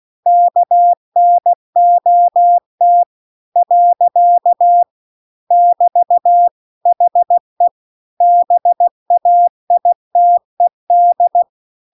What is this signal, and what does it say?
Morse code, heard as a keyed tone, about 16 wpm, 700 Hz. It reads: KNOT . <BT> HE BAITED
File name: CW-The-Big-Twoheaded-River-2-16WPM